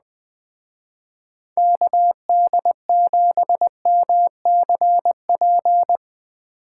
Morse code, heard as a keyed tone, about 20 wpm, 700 Hz. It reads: KD7MCP
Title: Speed Racing - USA 2x3 Call Signs 17wpm